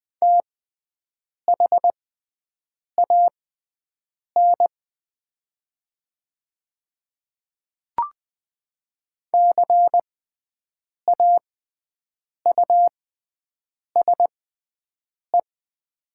Morse code, heard as a keyed tone, about 20 wpm, spaced as for 7 wpm, 700 Hz. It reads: THAN CAUSE
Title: 6x Character Spacing - Top 200 Words - 20wpm